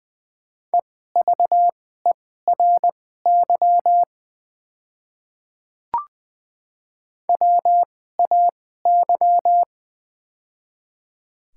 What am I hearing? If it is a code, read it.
Morse code, heard as a keyed tone, about 20 wpm, spaced as for 14 wpm, 700 Hz. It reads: EVERY WAY
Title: 2x Character Spacing - Top 200 Words - 20wpm